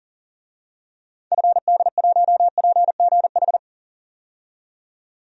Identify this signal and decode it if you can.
Morse code, heard as a keyed tone, about 40 wpm, 700 Hz. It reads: FD1PGH
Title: DXCC-100 Call Signs - 6 Characters - 40wpm